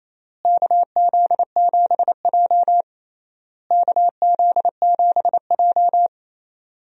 Morse code, heard as a keyed tone, about 27 wpm, 700 Hz. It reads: KZ7J KZ7J
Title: USA Call Signs - 4 Characters - Repeated 3x - 28wpm